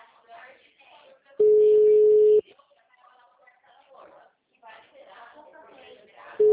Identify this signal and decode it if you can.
Ringback tone (Europe, 425 Hz), with speech.